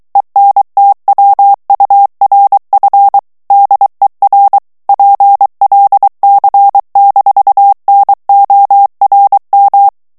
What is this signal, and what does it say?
Morse code, heard as a keyed tone, about 23 wpm, 800 Hz. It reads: ENTWURF DER PLC-NORM
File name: ALLEMAND texte clair 1 vitesse 22 mots minute